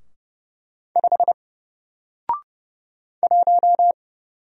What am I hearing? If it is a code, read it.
Morse code, heard as a keyed tone, about 30 wpm, 700 Hz. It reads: 5 1
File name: Single-Letter-Number-30wpm